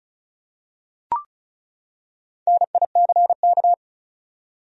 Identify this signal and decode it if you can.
Morse code, heard as a keyed tone, about 34 wpm, 700 Hz. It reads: NICK
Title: USA Common Names - Max 4 Letters - 35wpm